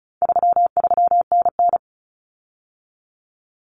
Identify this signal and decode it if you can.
Morse code, heard as a keyed tone, about 34 wpm, 700 Hz. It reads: 33NN
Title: Common Cut Numbers - 4-Digit 35wpm